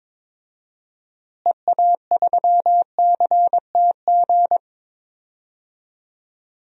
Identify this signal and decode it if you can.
Morse code, heard as a keyed tone, about 22 wpm, 700 Hz. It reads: EA3CTG
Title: DXCC-100 Call Signs - 6 Characters - Repeated 3x - 22wpm